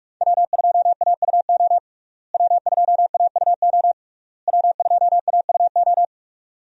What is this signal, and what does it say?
Morse code, heard as a keyed tone, about 46 wpm, 700 Hz. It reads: W2AUO W2AUO W2AUO
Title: USA Call Signs - 5 Characters - Repeated 3x - 45wpm